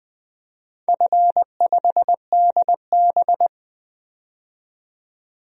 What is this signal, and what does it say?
Morse code, heard as a keyed tone, about 20 wpm, 700 Hz. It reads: F5DB
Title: DXCC-100 Call Signs - 4 Characters - 20wpm